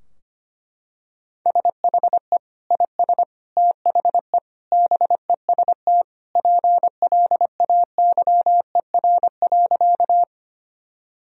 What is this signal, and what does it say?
Morse code, heard as a keyed tone, about 25 wpm, 700 Hz. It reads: SHE IS THE BEST PLAYER.
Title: Sentences from Top 1000 Words - Review Part 4 - 25wpm